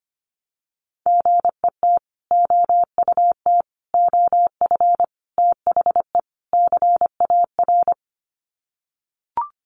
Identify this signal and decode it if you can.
Morse code, heard as a keyed tone, about 25 wpm, 700 Hz. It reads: GET OUT OF THE CAR
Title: Sets of 5 Words Encore2 25wpm